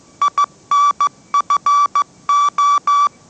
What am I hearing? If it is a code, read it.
Morse code, heard as a keyed tone, about 15 wpm, 1150 Hz. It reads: INFO
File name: mrx / mrx065 16wpm